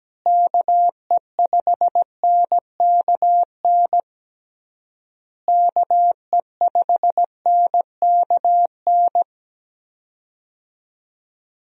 Morse code, heard as a keyed tone, about 17 wpm, 700 Hz. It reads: KE5NKN KE5NKN
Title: USA Call Signs - 6 Characters - Repeated 3x - 17wpm